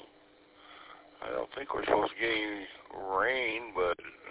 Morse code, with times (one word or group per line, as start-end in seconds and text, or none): none